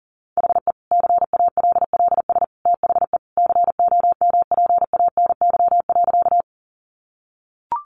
Morse code, heard as a keyed tone, about 40 wpm, 700 Hz. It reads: HE CALLS THE COMPANY.
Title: Additional Easy Phrases 40wpm